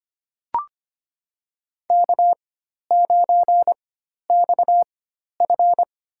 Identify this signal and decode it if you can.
Morse code, heard as a keyed tone, about 25 wpm, spaced as for 12 wpm, 700 Hz. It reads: K9XF
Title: USA Call Signs - 4 Characters - 4x Character Spacing - 25wpm